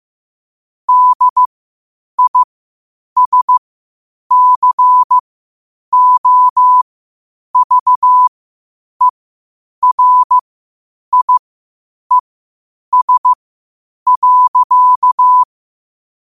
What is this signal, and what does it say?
Morse code, heard as a keyed tone, about 15 wpm, spaced as for 8 wpm, 1000 Hz. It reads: DISCOVERIES.
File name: qotdfiles / 8wpm